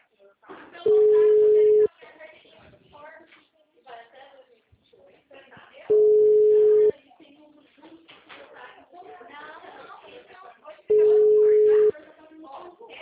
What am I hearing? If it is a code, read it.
Ringback tone (Europe, 425 Hz), with speech.
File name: out- -2030- - - .23909